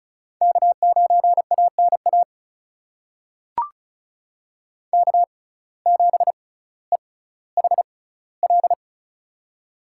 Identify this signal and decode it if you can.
Morse code, heard as a keyed tone, about 36 wpm, spaced as for 18 wpm, 700 Hz. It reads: K9ANA K7EHL
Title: USA Call Signs - 5 Characters - 6x Character Spacing - 35wpm